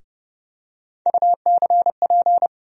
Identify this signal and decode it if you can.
Morse code, heard as a keyed tone, about 30 wpm, 700 Hz. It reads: UCP
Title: USA Call Sign Postfixes - 3 Letters 30wpm